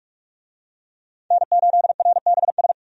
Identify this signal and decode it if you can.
Morse code, heard as a keyed tone, about 46 wpm, 700 Hz. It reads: N8RBS
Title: USA Call Signs - 5 Characters - 45wpm